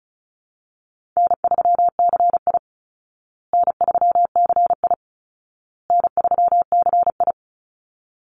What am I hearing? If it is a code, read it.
Morse code, heard as a keyed tone, about 34 wpm, 700 Hz. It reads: N3CI N3CI N3CI
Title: Easy Call Signs Repeated 3x - 35wpm